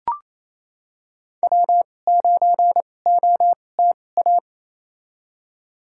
Morse code, heard as a keyed tone, about 28 wpm, spaced as for 20 wpm, 700 Hz. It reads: W9OTA
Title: USA Call Signs - 2x Character Spacing - 28wpm